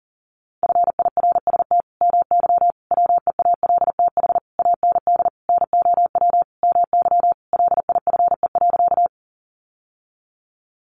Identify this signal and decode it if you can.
Morse code, heard as a keyed tone, about 40 wpm, 700 Hz. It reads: FIRST MY WEALTH AND NOW MY LIFE.
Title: Sentences from Top 1200 Words 40wpm